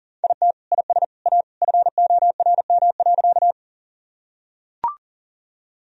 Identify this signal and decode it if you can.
Morse code, heard as a keyed tone, about 40 wpm, 700 Hz. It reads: IT IS A FORM.
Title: Sentences from Top 1000 Words - Review Part 1 - 40wpm